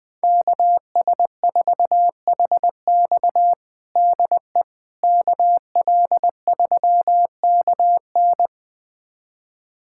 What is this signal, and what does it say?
Morse code, heard as a keyed tone, about 21 wpm, 700 Hz. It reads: KS4HX DE KL3KN
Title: USA Callsign DE Callsign - 5 Characters - 20wpm